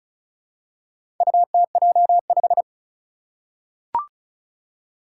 Morse code, heard as a keyed tone, about 36 wpm, 700 Hz. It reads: UTJ5
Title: Four Challenging Letter-Number Combinations 35wpm